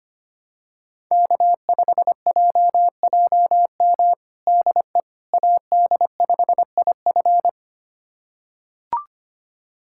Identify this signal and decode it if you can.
Morse code, heard as a keyed tone, about 25 wpm, 700 Hz. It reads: K5JJM DE AD5IF
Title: USA Callsign DE Callsign - 5 Characters - 25wpm